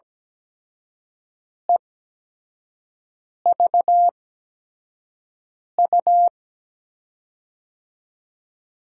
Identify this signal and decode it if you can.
Morse code, heard as a keyed tone, about 17 wpm, spaced as for 5 wpm, 700 Hz. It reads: EVU
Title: USA Call Signs - 5 Characters - 8x Character Spacing - 17wpm